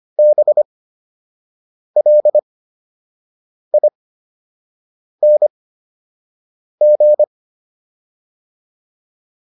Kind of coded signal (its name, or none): Morse code